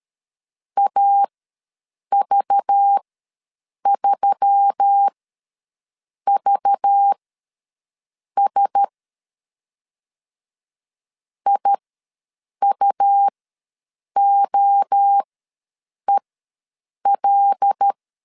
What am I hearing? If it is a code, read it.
Morse code, heard as a keyed tone, about 13 wpm, spaced as for 7 wpm, 800 Hz. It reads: AV3VS IUOEL
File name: LEZIONI / 5 LBVC / 5 LBV ws 12wpm 3015